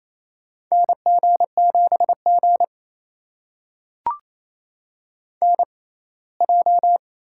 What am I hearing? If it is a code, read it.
Morse code, heard as a keyed tone, about 28 wpm, spaced as for 19 wpm, 700 Hz. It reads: NG7G NJ
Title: USA Call Signs - 4 Characters - 6x Character Spacing - 28wpm